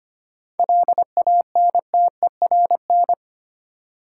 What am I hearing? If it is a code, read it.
Morse code, heard as a keyed tone, about 25 wpm, 690 Hz. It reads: LANTERN
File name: practice / 25wpm / Easy Words with Hints - 25wpm